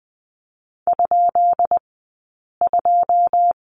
Morse code, heard as a keyed tone, about 20 wpm, 700 Hz. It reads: ? 2
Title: 165 Lesson 27d - Number 7 - Two Characters 20wpm